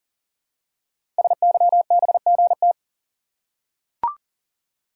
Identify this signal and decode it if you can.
Morse code, heard as a keyed tone, about 40 wpm, 700 Hz. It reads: SYBGT
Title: Code Groups - 5 Letters - 40wpm